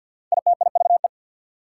Morse code, heard as a keyed tone, about 60 wpm, 690 Hz. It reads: ITIVE
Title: Speed-Racing Prefixes and Suffixes 50wpm